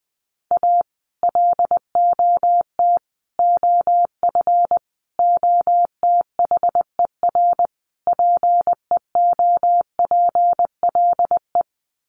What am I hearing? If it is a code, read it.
Morse code, heard as a keyed tone, about 20 wpm, 700 Hz. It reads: A LOT OF OTHER PEOPLE
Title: Sets of 5 Words Encore2 20wpm